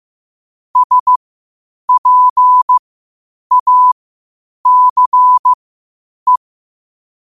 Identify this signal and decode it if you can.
Morse code, heard as a keyed tone, about 15 wpm, spaced as for 8 wpm, 1000 Hz. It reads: SPACE
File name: qotdfiles / 8wpm